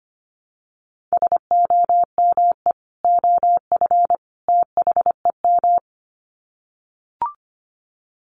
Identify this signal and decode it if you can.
Morse code, heard as a keyed tone, about 25 wpm, 700 Hz. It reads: SOME OF THEM
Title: Sets of 3 Words 25wpm